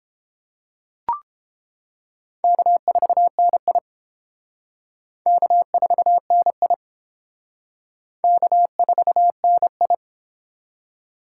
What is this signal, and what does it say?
Morse code, heard as a keyed tone, about 30 wpm, 700 Hz. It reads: K4NI K4NI K4NI